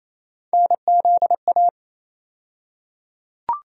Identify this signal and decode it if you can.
Morse code, heard as a keyed tone, about 28 wpm, 700 Hz. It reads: NZA
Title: USA Call Sign Postfixes - 3 Letters 28wpm